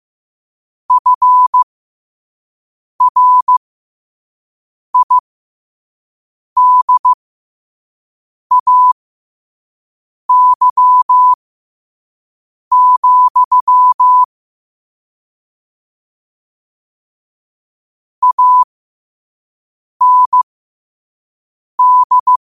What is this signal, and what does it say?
Morse code, heard as a keyed tone, about 15 wpm, spaced as for 5 wpm, 1000 Hz. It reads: FRIDAY, AND
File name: qotdfiles / 5wpm